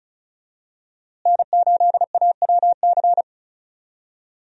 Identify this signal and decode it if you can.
Morse code, heard as a keyed tone, about 36 wpm, 700 Hz. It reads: N8AWC
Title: USA Call Signs - 5 Characters - Repeated 2x - 35wpm